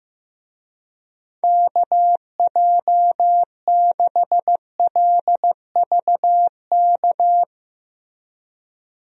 Morse code, heard as a keyed tone, about 15 wpm, 700 Hz. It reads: KJ6LVK
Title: USA Call Signs - 6 Characters - Repeated 3x - 15wpm